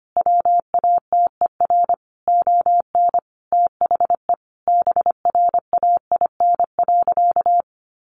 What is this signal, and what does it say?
Morse code, heard as a keyed tone, about 25 wpm, 700 Hz. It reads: WATER ON THE BRAIN.
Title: Sentences from Top 1100 Words 25wpm